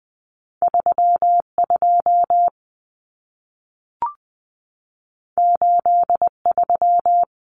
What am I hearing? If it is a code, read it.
Morse code, heard as a keyed tone, about 20 wpm, 700 Hz. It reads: 32 83
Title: Common Cut Numbers - 2-Digit 20wpm